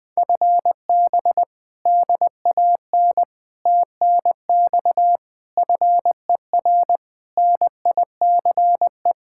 Morse code, heard as a keyed tone, about 21 wpm, 700 Hz. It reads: FB DAN TNX FER NICE
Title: Speed-Racing Common QSO Phrases - 20wpm